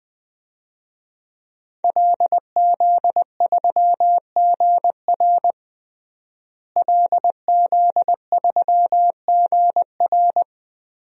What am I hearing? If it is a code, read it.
Morse code, heard as a keyed tone, about 20 wpm, 700 Hz. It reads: LZ3GR LZ3GR
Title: DXCC-100 Call Signs - 5 Characters - Repeated 3x - 20wpm